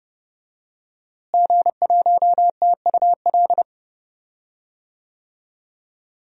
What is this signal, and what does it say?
Morse code, heard as a keyed tone, about 28 wpm, 700 Hz. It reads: G1TUL
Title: DXCC-100 Call Signs - 5 Characters - Repeated 3x - 30wpm